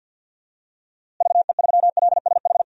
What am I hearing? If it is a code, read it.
Morse code, heard as a keyed tone, about 51 wpm, 700 Hz. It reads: VE3LSH
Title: DXCC-100 Call Signs - 6 Characters - Repeated 3x - 50wpm